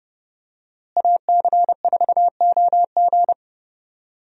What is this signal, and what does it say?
Morse code, heard as a keyed tone, about 31 wpm, 700 Hz. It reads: AC4OG
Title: USA Call Signs - 5 Characters - Repeated 3x - 30wpm